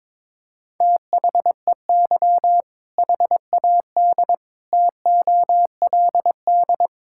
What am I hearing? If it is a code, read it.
Morse code, heard as a keyed tone, about 22 wpm, 700 Hz. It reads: THEY HAD TOLD
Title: Sentences from Top 1600 Words 22wpm